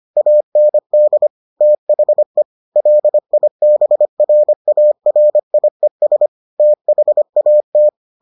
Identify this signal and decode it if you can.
Morse code, heard as a keyed tone, about 25 wpm, 600 Hz. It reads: AND THE LIBRARIES THAT